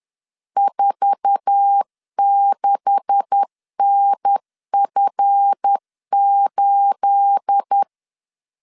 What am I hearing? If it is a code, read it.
Morse code, heard as a keyed tone, about 10 wpm, 800 Hz. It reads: 46NF8
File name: LEZIONI / 9 286 10wpm 2735